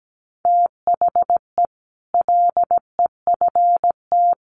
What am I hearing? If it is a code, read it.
Morse code, heard as a keyed tone, about 17 wpm, 700 Hz. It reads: THE LEFT
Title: Sets of 5 Words Encore 17wpm